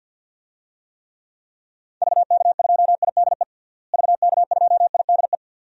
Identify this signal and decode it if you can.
Morse code, heard as a keyed tone, about 49 wpm, 700 Hz. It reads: VK2IDE VK2IDE
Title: DXCC-100 Call Signs - 6 Characters - Repeated 2x - 50wpm